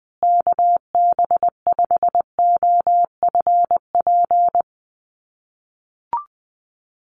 Morse code, heard as a keyed tone, about 20 wpm, 700 Hz. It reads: KB5OFP
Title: USA Call Signs Repeated 2x 20wpm